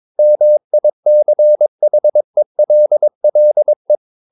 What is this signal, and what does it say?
Morse code, heard as a keyed tone, about 22 wpm, 600 Hz. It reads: MICHELLE
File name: Ferie QRS 22wpm